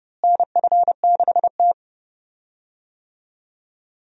Morse code, heard as a keyed tone, about 29 wpm, 700 Hz. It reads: NF6T
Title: USA Call Signs - 4 Characters - Repeated 2x - 30wpm